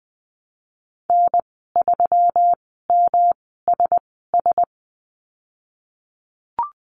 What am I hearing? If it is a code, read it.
Morse code, heard as a keyed tone, about 20 wpm, spaced as for 15 wpm, 700 Hz. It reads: N3MSS
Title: 173 Lesson 28g - Number 9 - Call Signs 20wpm